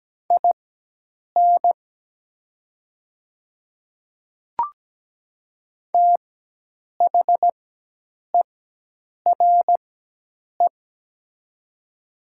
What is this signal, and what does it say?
Morse code, heard as a keyed tone, about 17 wpm, spaced as for 8 wpm, 700 Hz. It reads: IN THERE
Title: 4x Character Spacing - Top 100 Words - 17wpm